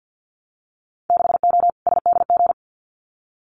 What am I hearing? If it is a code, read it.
Morse code, heard as a keyed tone, about 50 wpm, 700 Hz. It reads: 6O SDG
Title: Winter Field Day Exchanges Encore 50wpm